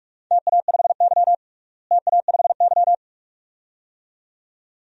Morse code, heard as a keyed tone, about 43 wpm, 700 Hz. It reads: TA5Y TA5Y
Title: DXCC-100 Call Signs - 4 Characters - Repeated 2x - 45wpm